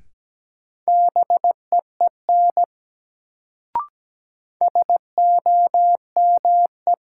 Morse code, heard as a keyed tone, about 17 wpm, 700 Hz. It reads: BEEN SOME
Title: Sets of 2 Words - New Word Drill - 17wpm